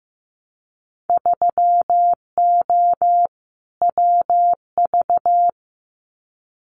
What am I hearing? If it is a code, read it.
Morse code, heard as a keyed tone, about 15 wpm, 700 Hz. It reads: 3O WV
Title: Winter Field Day Exchanges 15wpm